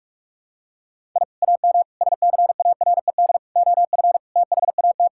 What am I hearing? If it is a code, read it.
Morse code, heard as a keyed tone, about 43 wpm, 700 Hz. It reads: I AM SCARED OF THAT
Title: Sentences from Top 1000 Words - Review Part 5 - 45wpm